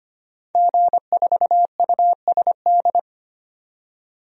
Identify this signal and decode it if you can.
Morse code, heard as a keyed tone, about 25 wpm, 690 Hz. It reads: G4USD
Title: DXCC-100 Call Signs - 5 Characters - 25wpm